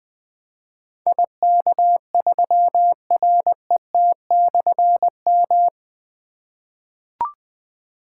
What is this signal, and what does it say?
Morse code, heard as a keyed tone, about 20 wpm, 700 Hz. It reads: IK3RET/M
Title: Difficult Call Signs 20wpm